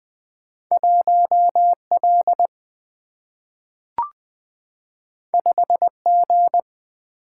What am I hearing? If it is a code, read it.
Morse code, heard as a keyed tone, about 20 wpm, 700 Hz. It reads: 1L 5G
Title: Single-Letter Single-Number 20wpm